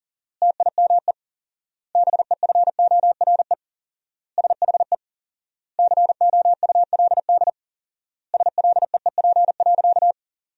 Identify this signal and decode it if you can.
Morse code, heard as a keyed tone, about 37 wpm, 700 Hz. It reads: TIME BEFORE SHE COULD SLEEP.
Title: 4x Word-Spacing - Sentences from Top 500 Words - 40wpm